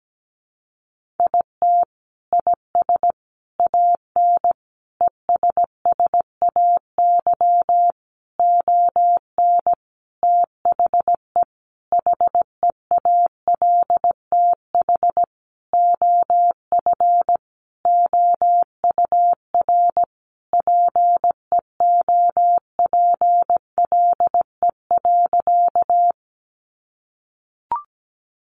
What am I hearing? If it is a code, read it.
Morse code, heard as a keyed tone, about 17 wpm, 700 Hz. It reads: IT IS AN ESSAY ON THE HEALTH OF OUR PEOPLE.
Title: Sentences from Top 1500 Words 17wpm